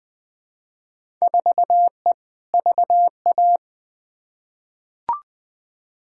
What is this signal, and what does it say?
Morse code, heard as a keyed tone, about 20 wpm, 700 Hz. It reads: 4E VA
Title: Field Day Exchanges Encore - 20wpm